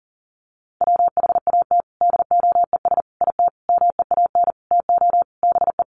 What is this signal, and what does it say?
Morse code, heard as a keyed tone, about 40 wpm, 700 Hz. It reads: WHAT DOES IT MEAN TO BE
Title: Sentences from Top 1400 Words 40wpm